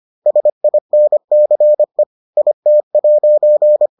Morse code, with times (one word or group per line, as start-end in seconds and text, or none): 0.3-2.0 SINCE
2.4-3.9 IT'